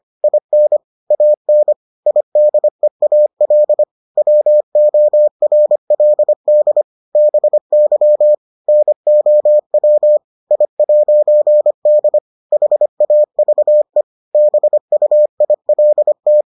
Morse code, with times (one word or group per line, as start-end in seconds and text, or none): 0.2-0.8 IN
1.1-1.7 AN
2.1-3.8 IDEAL
4.2-6.8 WORLD
7.2-8.3 BY
8.7-10.2 NOW
10.5-12.2 I'D
12.5-14.0 HAVE
14.3-16.4 BUILT